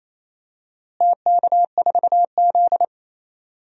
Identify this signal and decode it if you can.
Morse code, heard as a keyed tone, about 28 wpm, 700 Hz. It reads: TK4Z
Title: DXCC-100 Call Signs - 4 Characters - Repeated 2x - 28wpm